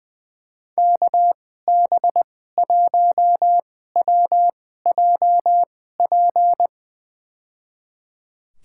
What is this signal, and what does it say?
Morse code, heard as a keyed tone, about 20 wpm, spaced as for 14 wpm, 700 Hz. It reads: KB1WJP
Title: USA Call Signs - 6 Characters - 2x Character Spacing - 20wpm